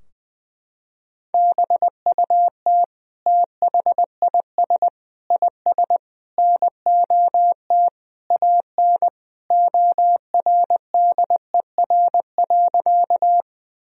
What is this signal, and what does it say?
Morse code, heard as a keyed tone, about 21 wpm, 700 Hz. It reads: BUT THIS IS NOT AN ORDER.
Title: Sentences from Top 1000 Words - Review Part 2 - 20wpm